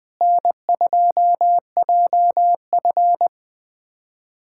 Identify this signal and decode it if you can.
Morse code, heard as a keyed tone, about 20 wpm, 700 Hz. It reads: N2JF
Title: USA Callsign DE Callsign - 4 Characters - 20wpm